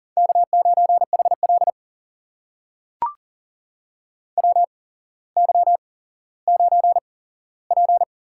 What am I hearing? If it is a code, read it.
Morse code, heard as a keyed tone, about 40 wpm, spaced as for 17 wpm, 700 Hz. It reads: K9HL WY9P
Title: USA Call Signs - 4 Characters - 8x Character Spacing - 40wpm